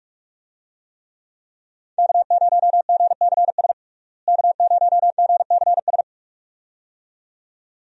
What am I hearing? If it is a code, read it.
Morse code, heard as a keyed tone, about 46 wpm, 700 Hz. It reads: K0GCS K0GCS
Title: USA Call Signs - 5 Characters - Repeated 2x - 45wpm